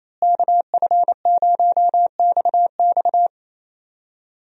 Morse code, heard as a keyed tone, about 28 wpm, 700 Hz. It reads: KF0XX
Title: USA Call Signs - 5 Characters - Repeated 2x - 28wpm